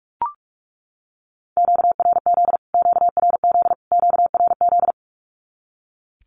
Morse code, heard as a keyed tone, about 44 wpm, 700 Hz. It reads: QRZ QRZ QRZ
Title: Q-Codes Repeated 3x 45wpm